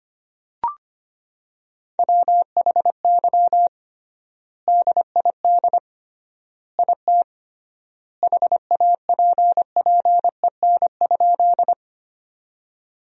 Morse code, heard as a keyed tone, about 25 wpm, 700 Hz. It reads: WHY DID IT HAPPEN?
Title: 3x Word-Spacing - Sentences from Top 400 Words - 25wpm